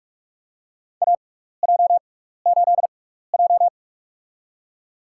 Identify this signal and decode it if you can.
Morse code, heard as a keyed tone, about 44 wpm, spaced as for 15 wpm, 700 Hz. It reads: AJ8J
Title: USA Call Signs - 4 Characters - 6x Character Spacing - 45wpm